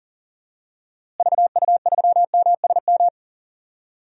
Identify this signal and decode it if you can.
Morse code, heard as a keyed tone, about 37 wpm, 700 Hz. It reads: VU3MSM
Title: DXCC-100 Call Signs - 6 Characters - Repeated 3x - 40wpm